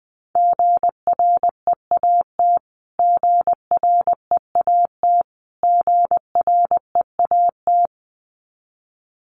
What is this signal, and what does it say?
Morse code, heard as a keyed tone, about 20 wpm, 700 Hz. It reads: GREAT GREAT GREAT
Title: Top 500 Words Repeated 20wpm